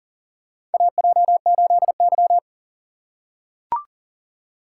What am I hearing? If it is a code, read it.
Morse code, heard as a keyed tone, about 40 wpm, 700 Hz. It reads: AJ8Y
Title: USA Call Signs - 4 Characters - 4x Character Spacing - 40wpm